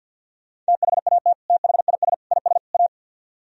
Morse code, heard as a keyed tone, about 64 wpm, 700 Hz. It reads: THAT THIS IS A